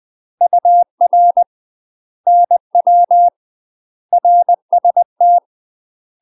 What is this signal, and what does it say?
Morse code, heard as a keyed tone, about 20 wpm, 700 Hz. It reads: UR NW RST
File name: UK MASTER QSOs TRAINING / 20WPM / UK MASTER QSOs 0010